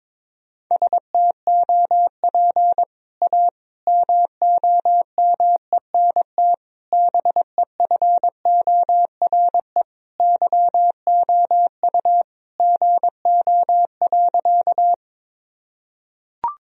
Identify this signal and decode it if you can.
Morse code, heard as a keyed tone, about 22 wpm, 700 Hz. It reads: STOP A MOMENT BEFORE YOU GO.
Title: Sentences from Top 1000 Words - Review Part 1 - 22wpm